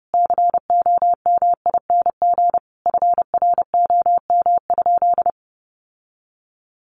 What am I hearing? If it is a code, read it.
Morse code, heard as a keyed tone, about 30 wpm, 700 Hz. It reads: COMING FROM?
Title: Sentences from Top 1400 Words 30wpm